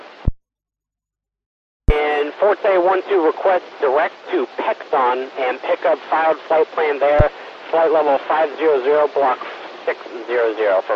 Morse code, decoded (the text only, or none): none